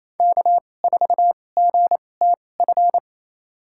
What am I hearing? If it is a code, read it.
Morse code, heard as a keyed tone, about 28 wpm, spaced as for 20 wpm, 700 Hz. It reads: K4GTF
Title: USA Call Signs - 2x Character Spacing - 28wpm